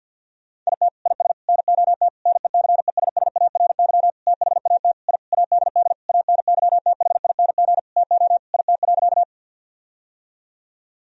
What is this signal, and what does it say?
Morse code, heard as a keyed tone, about 48 wpm, 700 Hz. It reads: IT IS NOT NECESSARY THAT I ADD ANYTHING TO IT.
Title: Sentences from Top 1000 Words - Review Part 1 - 50wpm